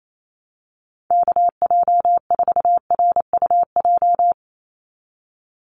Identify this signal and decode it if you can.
Morse code, heard as a keyed tone, about 28 wpm, 700 Hz. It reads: KJ4RUJ
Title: USA Call Signs 28wpm